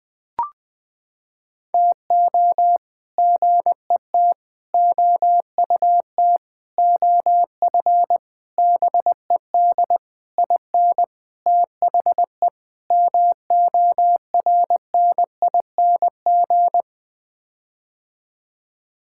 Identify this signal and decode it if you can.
Morse code, heard as a keyed tone, about 20 wpm, 700 Hz. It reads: TO GET OUT OF BED IN THE MORNING
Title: Sets of 8 Words - 20wpm